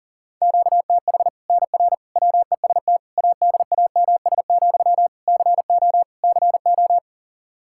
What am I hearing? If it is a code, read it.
Morse code, heard as a keyed tone, about 40 wpm, 700 Hz. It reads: QTH NR WEST ADAMS, CO CO
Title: Practice QSO - Exchange 1 - Beginning - 40wpm